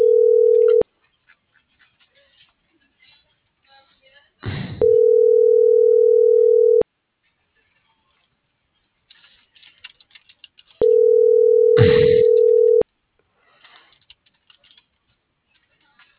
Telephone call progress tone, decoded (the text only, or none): ringback tone